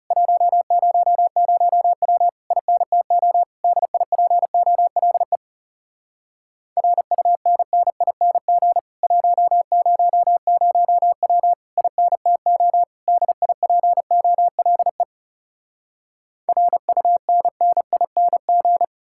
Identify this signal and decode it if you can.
Morse code, heard as a keyed tone, about 40 wpm, 700 Hz. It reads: 100W INTO DIPOLE RUNNING 100W INTO DIPOLE RUNNING